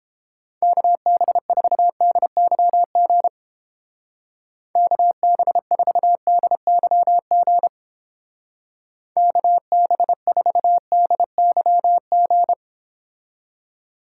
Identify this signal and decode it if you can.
Morse code, heard as a keyed tone, about 30 wpm, 700 Hz. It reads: KB4DYG KB4DYG KB4DYG